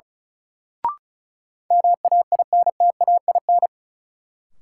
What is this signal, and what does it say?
Morse code, heard as a keyed tone, about 34 wpm, 700 Hz. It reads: MAINTAIN
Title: Sentences from Top 1000 Words - Review Part 5 - New Word Drill - 35wpm